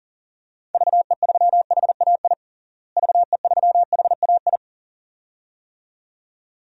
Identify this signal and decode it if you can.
Morse code, heard as a keyed tone, about 40 wpm, 700 Hz. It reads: VE3HAI VE3HAI
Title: DXCC-100 Call Signs - 6 Characters - Repeated 2x - 40wpm